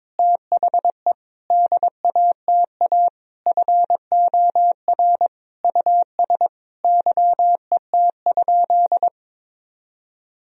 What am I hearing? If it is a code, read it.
Morse code, heard as a keyed tone, about 22 wpm, 700 Hz. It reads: THE DATA FOR US YET?
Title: Sentences from Top 1000 Words - Review Part 3 - 22wpm